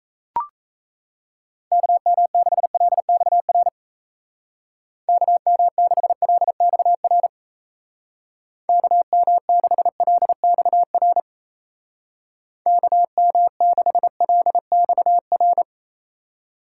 Morse code, heard as a keyed tone, about 36 wpm, 700 Hz. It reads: KM6LXR KM6LXR KM6LXR KM6LXR